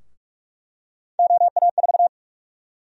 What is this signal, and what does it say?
Morse code, heard as a keyed tone, about 46 wpm, 700 Hz. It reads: OA4
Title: USA Call Sign Prefixes - Repeated 2x - 45wpm